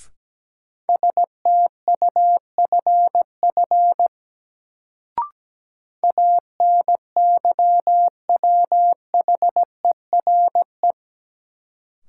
Morse code, heard as a keyed tone, about 17 wpm, 700 Hz. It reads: STUFF ANYWHERE
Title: Sentences from Top 1400 Words - New Word Drill - 17wpm